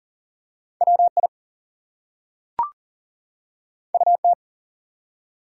Morse code, heard as a keyed tone, about 42 wpm, 700 Hz. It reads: WI UT
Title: US States and Canadian Provinces 40wpm